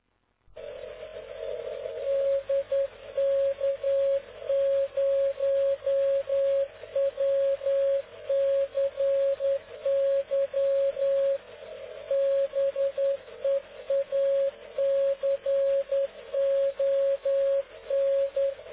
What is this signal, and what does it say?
Morse code, heard as a keyed tone, about 10 wpm, 550 Hz. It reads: DK0WCY BEACON
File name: cw / ubacw dk0wcy slow